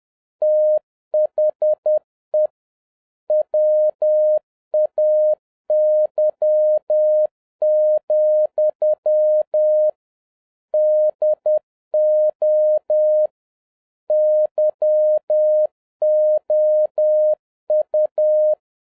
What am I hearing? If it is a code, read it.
Morse code, heard as a keyed tone, about 10 wpm, 610 Hz. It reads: THE WAY, DO YOU